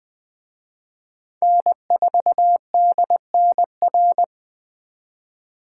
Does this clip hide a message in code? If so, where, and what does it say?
Yes: Morse code from 1.4 to 4.2 s, heard as a keyed tone, about 19 wpm, 700 Hz. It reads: N4DNR